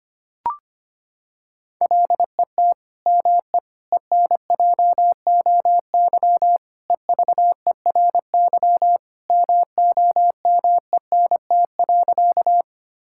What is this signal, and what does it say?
Morse code, heard as a keyed tone, about 25 wpm, 700 Hz. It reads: LET ME ENJOY EVERY MOMENT.
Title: Sentences from Top 1000 Words - Review Part 5 - 25wpm